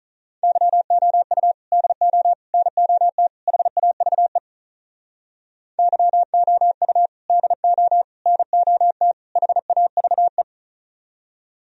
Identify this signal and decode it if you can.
Morse code, heard as a keyed tone, about 45 wpm, 700 Hz. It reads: YOU DO NOT HAVE YOU DO NOT HAVE